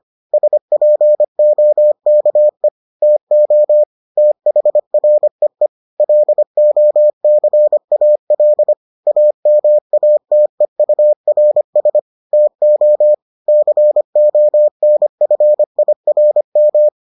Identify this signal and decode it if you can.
Morse code, heard as a keyed tone, about 25 wpm, 600 Hz. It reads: SPOKE TO THREE LOCAL AMATEURS TO CONFIRM